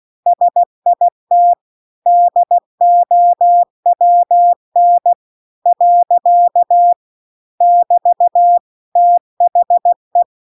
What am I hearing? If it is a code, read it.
Morse code, heard as a keyed tone, about 16 wpm, 700 Hz. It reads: SIT DOWN . <BT> THE
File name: CW-The-Big-Twoheaded-River-2-16WPM